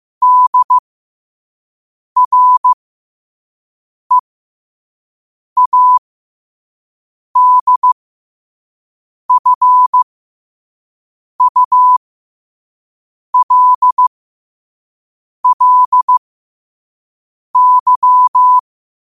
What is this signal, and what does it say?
Morse code, heard as a keyed tone, about 15 wpm, spaced as for 5 wpm, 1000 Hz. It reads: DREADFULLY
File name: qotdfiles / 5wpm